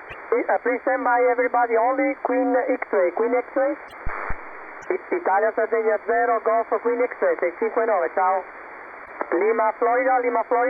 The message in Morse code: none